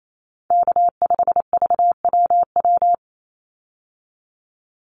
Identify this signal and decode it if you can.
Morse code, heard as a keyed tone, about 28 wpm, 700 Hz. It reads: K5VWW
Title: Call Signs Repeated 2x - 28wpm